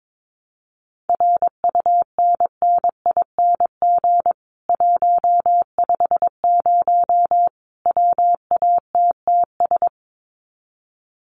Morse code, heard as a keyed tone, about 22 wpm, 700 Hz. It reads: RUNNING 150 WATTS
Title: Common QSO Phrases 22wpm